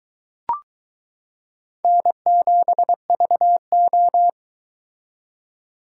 Morse code, heard as a keyed tone, about 22 wpm, 700 Hz. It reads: N7VO